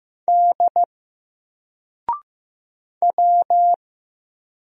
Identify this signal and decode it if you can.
Morse code, heard as a keyed tone, about 15 wpm, 700 Hz. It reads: D W
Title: Single Letter-Number 15wpm